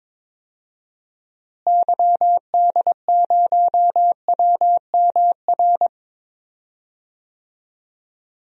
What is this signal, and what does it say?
Morse code, heard as a keyed tone, about 21 wpm, 700 Hz. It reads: YD0WMR
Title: DXCC-100 Call Signs - 6 Characters - Repeated 3x - 22wpm